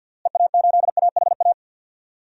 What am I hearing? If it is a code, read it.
Morse code, heard as a keyed tone, about 53 wpm, 700 Hz. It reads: EA8AHA
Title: DXCC-100 Call Signs - 6 Characters - Repeated 2x - 50wpm